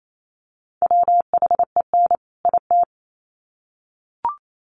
Morse code, heard as a keyed tone, about 28 wpm, 700 Hz. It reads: WHEN IT
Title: Sets of 2 Words 28wpm